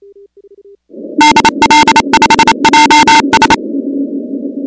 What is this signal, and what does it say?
Morse code, heard as a keyed tone, about 28 wpm, 900 Hz, with noise. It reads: DL5JS
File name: contests / 2019-cw-wpx-kt5j-k5pi- -2330 0-16